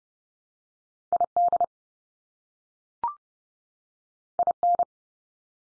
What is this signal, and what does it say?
Morse code, heard as a keyed tone, about 30 wpm, 700 Hz. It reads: ID IN
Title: US States 30WPM